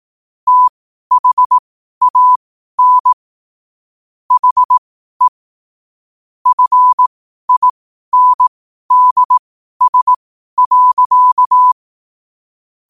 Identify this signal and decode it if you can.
Morse code, heard as a keyed tone, about 18 wpm, spaced as for 12 wpm, 1000 Hz. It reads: THAN HE FINDS.
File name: qotdfiles / 12wpm